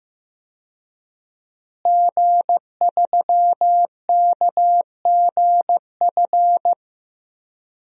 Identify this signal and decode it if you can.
Morse code, heard as a keyed tone, about 15 wpm, 700 Hz. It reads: G3KGF
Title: DXCC-100 Call Signs - 5 Characters - Repeated 2x - 15wpm